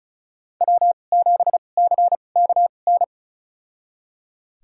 Morse code, heard as a keyed tone, about 34 wpm, spaced as for 25 wpm, 700 Hz. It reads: W7CKN
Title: USA Call Signs - 5 Characters - 2x Character Spacing - 35wpm